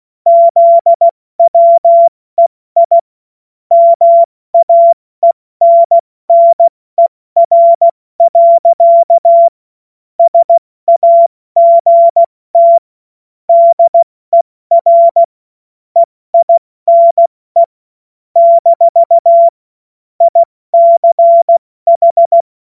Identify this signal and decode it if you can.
Morse code, heard as a keyed tone, about 16 wpm, 680 Hz. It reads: ZWEI MAENNER. SAGT DER EINE - ICH
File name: Morsekurs / Witze / Witz 26. April 2022 70 F80